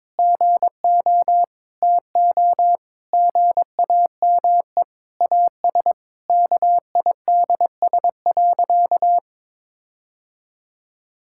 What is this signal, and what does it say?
Morse code, heard as a keyed tone, about 22 wpm, 700 Hz. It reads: GO TO GAME AS KIDS.
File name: Difficult Sentences with Hints - 22wpm